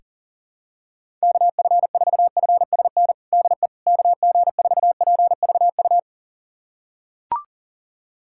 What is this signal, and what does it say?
Morse code, heard as a keyed tone, about 37 wpm, 700 Hz. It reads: KF4FSN DE KG4PVU
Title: USA Callsign DE Callsign - 6 Characters - 40wpm